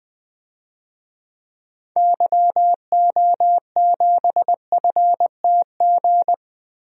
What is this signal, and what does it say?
Morse code, heard as a keyed tone, about 20 wpm, 700 Hz. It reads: YO7FTG
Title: DXCC-100 Call Signs - 6 Characters - Repeated 3x - 20wpm